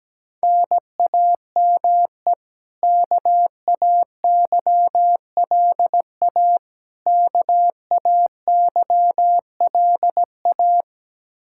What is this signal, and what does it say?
Morse code, heard as a keyed tone, about 17 wpm, 700 Hz. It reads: NAME KAYLA KAYLA
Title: Practice QSO - Exchange 1 - Beginning - 17wpm